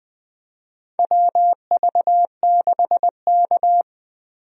Morse code, heard as a keyed tone, about 20 wpm, 700 Hz. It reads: WV6K
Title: USA Call Signs - 4 Characters - Repeated 2x - 20wpm